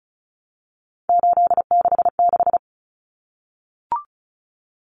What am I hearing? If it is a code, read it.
Morse code, heard as a keyed tone, about 34 wpm, 700 Hz. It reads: 866
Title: Common Cut Numbers - 3-Digit 35wpm